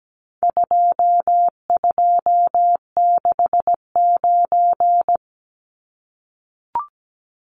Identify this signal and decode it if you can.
Morse code, heard as a keyed tone, about 17 wpm, 700 Hz. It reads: 2269
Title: Numbers - 4-Digit 17wpm